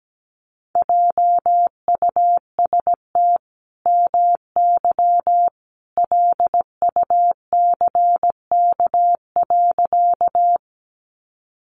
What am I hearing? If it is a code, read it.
Morse code, heard as a keyed tone, about 17 wpm, 700 Hz. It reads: JUST MY LUCK.
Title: Sentences from Top 1500 Words 17wpm